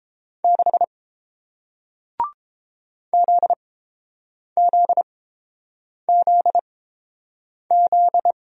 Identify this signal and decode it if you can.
Morse code, heard as a keyed tone, about 30 wpm, spaced as for 7 wpm, 700 Hz. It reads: 6 ZZZZ